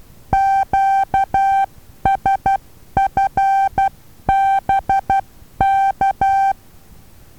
Morse code, heard as a keyed tone, about 12 wpm, 780 Hz. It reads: QSFBK
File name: Taviro HA6VA B